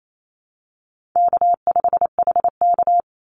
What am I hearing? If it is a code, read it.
Morse code, heard as a keyed tone, about 28 wpm, 700 Hz. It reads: K5HK
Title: Easy Call Signs Repeated 2x - 28wpm